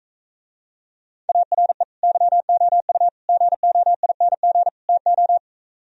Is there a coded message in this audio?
Yes: Morse code from 1.3 to 5.4 s, heard as a keyed tone, about 45 wpm, 700 Hz. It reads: ARE YOU GOING TO